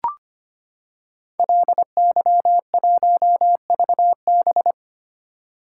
Morse code, heard as a keyed tone, about 25 wpm, 700 Hz. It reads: LY1VB
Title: Code Groups - 5 Letter:Numbers - 25wpm